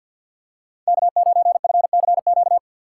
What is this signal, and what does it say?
Morse code, heard as a keyed tone, about 53 wpm, 700 Hz. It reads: K9FCQ
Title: USA Call Signs - 5 Characters - 2x Character Spacing - 50wpm